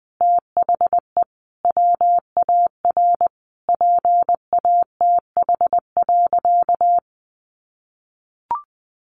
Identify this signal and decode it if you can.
Morse code, heard as a keyed tone, about 20 wpm, 700 Hz. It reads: THE WAR PATH.
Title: Sentences from Top 1400 Words 20wpm